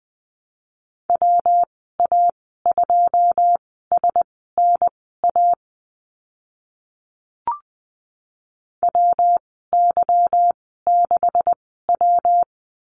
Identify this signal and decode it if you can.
Morse code, heard as a keyed tone, about 20 wpm, spaced as for 14 wpm, 700 Hz. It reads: WA2SNA WY6W
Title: 161 Lesson 26g - Letter g - Call Signs 20wpm